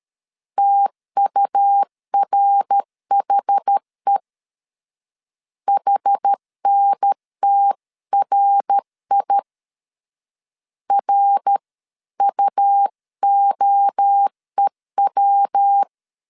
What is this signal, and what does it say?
Morse code, heard as a keyed tone, about 13 wpm, 800 Hz. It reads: TURHE HNTRI RUOEW
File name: CW / DET / 1 3 1500C S A 12wpm 2905